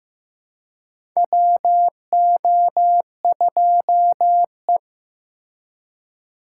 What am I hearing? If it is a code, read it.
Morse code, heard as a keyed tone, about 15 wpm, 700 Hz. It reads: WO2E
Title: K1USN SST QSO Practice - 15wpm